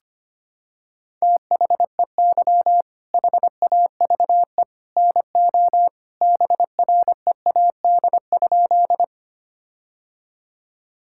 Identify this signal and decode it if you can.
Morse code, heard as a keyed tone, about 25 wpm, 700 Hz. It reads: THEY HAVE NO BREAD?
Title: Sentences from Top 1600 Words 25wpm